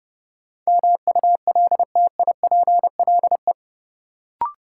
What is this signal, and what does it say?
Morse code, heard as a keyed tone, about 30 wpm, 700 Hz. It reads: MULTIPLE
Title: Sets of 5 Words - New Word Drill - 30wpm